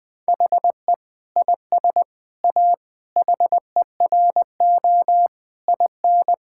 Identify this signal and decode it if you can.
Morse code, heard as a keyed tone, about 21 wpm, 700 Hz. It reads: HE IS A HERO IN
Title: Sentences with 6-Word Phrases - 20wpm